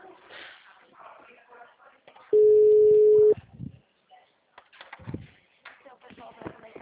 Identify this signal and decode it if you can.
Ringback tone (Europe, 425 Hz), with speech.
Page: Cordeiropolis 3